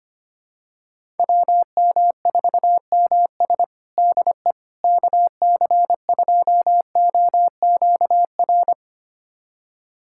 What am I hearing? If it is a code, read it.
Morse code, heard as a keyed tone, about 25 wpm, 700 Hz. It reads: WM4MS DE KC2OQR
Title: USA Callsign DE Callsign 25wpm